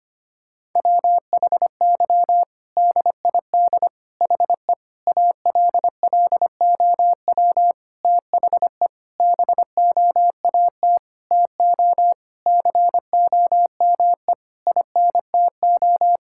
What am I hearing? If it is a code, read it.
Morse code, heard as a keyed tone, about 25 wpm, 700 Hz. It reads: WHY DID HE ALLOW THE BOAT TO COME INTO
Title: Sentences from Top 1000 Words - Review Part 5 - 25wpm